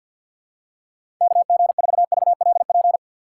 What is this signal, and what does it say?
Morse code, heard as a keyed tone, about 53 wpm, 700 Hz. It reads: KG4VLP
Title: USA Call Signs - 6 Characters - 3x Character Spacing - 50wpm